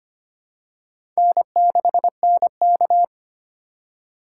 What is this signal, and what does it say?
Morse code, heard as a keyed tone, about 25 wpm, 700 Hz. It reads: N6NK
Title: USA Call Signs - 3x Character Spacing - 25wpm